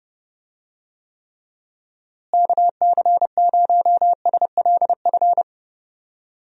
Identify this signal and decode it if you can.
Morse code, heard as a keyed tone, about 32 wpm, 700 Hz. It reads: KC0SLF
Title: USA Call Signs - 6 Characters - Repeated 3x - 30wpm